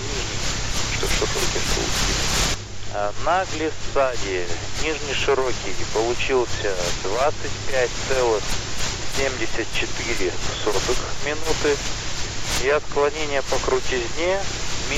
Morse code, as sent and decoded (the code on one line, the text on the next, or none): none